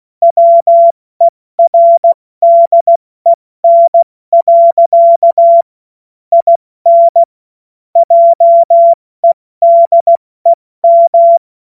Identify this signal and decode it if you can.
Morse code, heard as a keyed tone, about 16 wpm, 680 Hz. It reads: WERDEN. IN JEDEM
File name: Morsekurs / Witze / Witz 21. Juni 2022 70 F80